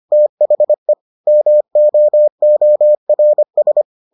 Morse code, heard as a keyed tone, about 25 wpm, 600 Hz. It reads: THE MOORS